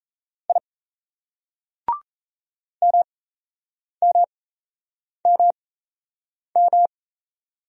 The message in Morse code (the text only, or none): I MMMM